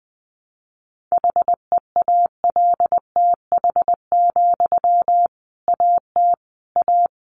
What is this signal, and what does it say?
Morse code, heard as a keyed tone, about 20 wpm, 700 Hz. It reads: HEALTH, AT A
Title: Sentences from Top 600 Words 20wpm